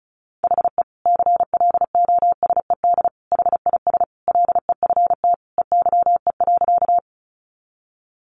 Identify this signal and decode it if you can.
Morse code, heard as a keyed tone, about 34 wpm, 700 Hz. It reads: HE CLOSED HIS LEFT EYE.
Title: Sentences from Top 1400 Words 35wpm